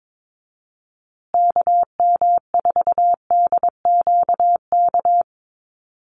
Morse code, heard as a keyed tone, about 22 wpm, 700 Hz. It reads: KM4DQK
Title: USA Call Signs Repeated 2x 22wpm